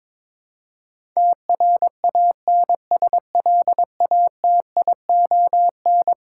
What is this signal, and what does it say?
Morse code, heard as a keyed tone, about 22 wpm, 700 Hz. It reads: TRANSLATION
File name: practice / 22wpm / Eleven Letter Words - 22wpm